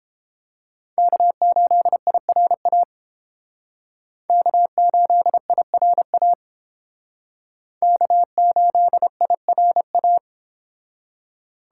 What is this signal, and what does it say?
Morse code, heard as a keyed tone, about 30 wpm, 700 Hz. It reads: K8IRA K8IRA K8IRA